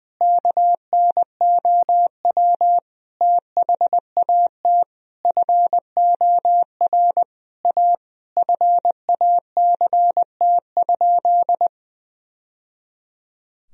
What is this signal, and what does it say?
Morse code, heard as a keyed tone, about 20 wpm, 700 Hz. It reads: KNOW THAT FOR A FACT?
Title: Sentences from Top 1000 Words - Review Part 3 - 20wpm